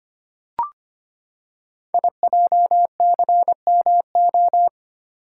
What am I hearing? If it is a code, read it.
Morse code, heard as a keyed tone, about 25 wpm, 700 Hz. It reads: IJCMO
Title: Code Groups - 5 Letters - 25wpm